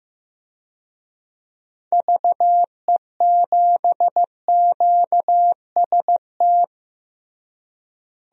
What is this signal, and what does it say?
Morse code, heard as a keyed tone, about 15 wpm, 700 Hz. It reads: VE7QST
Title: DXCC-100 Call Signs - 6 Characters - Repeated 3x - 15wpm